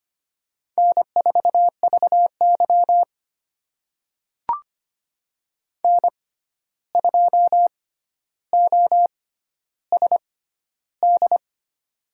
Morse code, heard as a keyed tone, about 25 wpm, spaced as for 12 wpm, 700 Hz. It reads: N4VY N2OSD
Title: USA Call Signs - 6x Character Spacing - 25wpm